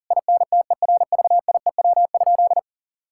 Morse code, heard as a keyed tone, about 37 wpm, 700 Hz. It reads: INTERVIEW?
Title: Sentences from Top 1600 Words 40wpm